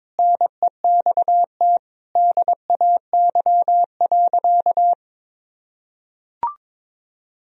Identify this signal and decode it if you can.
Morse code, heard as a keyed tone, about 21 wpm, 700 Hz. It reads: NEXT DAY.
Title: Sentences from Top 1000 Words - Review Part 1 - 22wpm